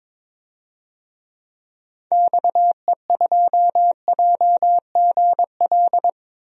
Morse code, heard as a keyed tone, about 22 wpm, 700 Hz. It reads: XE2JGL
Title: DXCC-100 Call Signs - 6 Characters - Repeated 3x - 22wpm